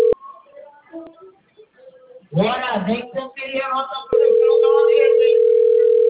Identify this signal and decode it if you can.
Ringback tone (North America), 440 + 480 Hz.